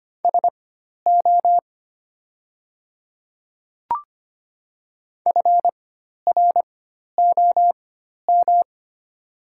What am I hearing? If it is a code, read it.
Morse code, heard as a keyed tone, about 25 wpm, spaced as for 12 wpm, 700 Hz. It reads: SO FROM
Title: 4x Character Spacing - Top 100 Words - 25wpm